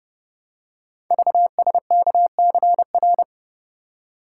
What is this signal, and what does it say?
Morse code, heard as a keyed tone, about 30 wpm, 700 Hz. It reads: VSKCR
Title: Code Groups - 5 Letters - 30wpm